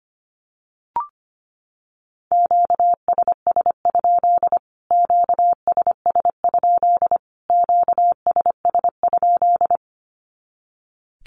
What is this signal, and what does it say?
Morse code, heard as a keyed tone, about 25 wpm, 700 Hz. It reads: QSS? QSS? QSS?
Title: Q-Codes Repeated 3x 25wpm